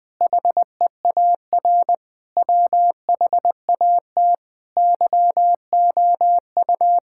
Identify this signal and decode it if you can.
Morse code, heard as a keyed tone, about 19 wpm, 700 Hz. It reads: HEAR WHAT YOU
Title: Sets of 8 Words - 20wpm